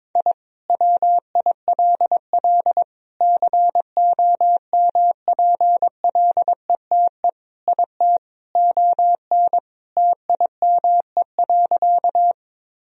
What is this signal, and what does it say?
Morse code, heard as a keyed tone, about 22 wpm, 700 Hz. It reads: I WILL COMPLETE IT ON TIME.
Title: Sentences from Top 1000 Words - Review Part 2 - 22wpm